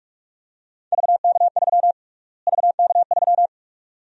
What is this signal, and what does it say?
Morse code, heard as a keyed tone, about 41 wpm, 690 Hz. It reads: VK3 VK3
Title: USA Call Sign Prefixes - Repeated 2x - 45wpm